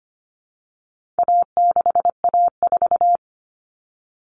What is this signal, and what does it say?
Morse code, heard as a keyed tone, about 25 wpm, 700 Hz. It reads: A6A4
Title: All Cut Numbers - 4-Digit 25wpm